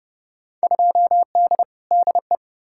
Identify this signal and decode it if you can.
Morse code, heard as a keyed tone, about 31 wpm, 700 Hz. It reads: 2D DE
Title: Field Day Exchanges Encore - 30wpm